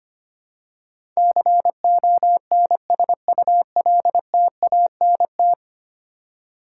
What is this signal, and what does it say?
Morse code, heard as a keyed tone, about 25 wpm, 700 Hz. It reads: CONSULTANT
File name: Ten Letter Words - 25wpm